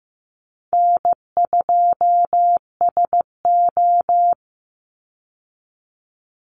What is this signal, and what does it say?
Morse code, heard as a keyed tone, about 15 wpm, 700 Hz. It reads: N2SO
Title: Easy Call Signs Repeated 2x - 15wpm